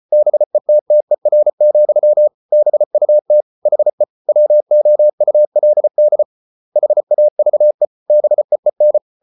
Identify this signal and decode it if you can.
Morse code, heard as a keyed tone, about 34 wpm, 600 Hz. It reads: BETTER, BUT HE WOULD HAVE BEEN